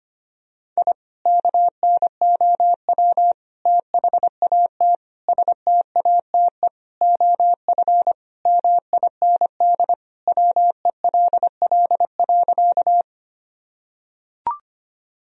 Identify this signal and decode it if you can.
Morse code, heard as a keyed tone, about 25 wpm, 700 Hz. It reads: I KNOW THAT STATE OF MIND WELL.
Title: Sentences from Top 1000 Words - Review Part 2 - 25wpm